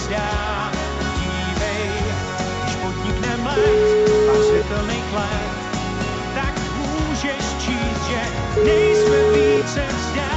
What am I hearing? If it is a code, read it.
Ringback tone (Europe, 425 Hz), with noise.